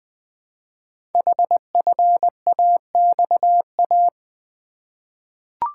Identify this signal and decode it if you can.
Morse code, heard as a keyed tone, about 20 wpm, 700 Hz. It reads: HFAXA
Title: Code Groups - 5 Letters - 20wpm